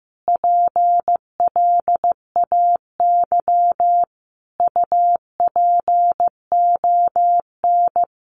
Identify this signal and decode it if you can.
Morse code, heard as a keyed tone, about 14 wpm, 700 Hz. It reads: PLAY UPON
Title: Sentences from Top 1500 Words 15wpm